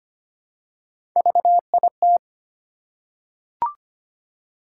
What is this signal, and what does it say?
Morse code, heard as a keyed tone, about 25 wpm, 700 Hz. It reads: VIT
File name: Three Letter-Number Combinations - 25wpm